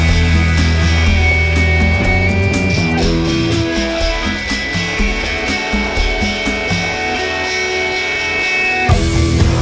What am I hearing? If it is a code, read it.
Dial-up modem handshake.